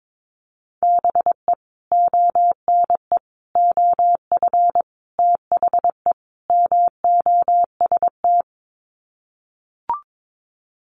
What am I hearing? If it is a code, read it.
Morse code, heard as a keyed tone, about 22 wpm, 700 Hz. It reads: BE ONE OF THE MOST
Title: Sets of 5 Words Encore 22wpm